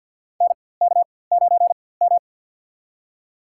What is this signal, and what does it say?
Morse code, heard as a keyed tone, about 47 wpm, spaced as for 23 wpm, 690 Hz. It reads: NK9M
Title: USA Call Signs - 4x Character Spacing - 50wpm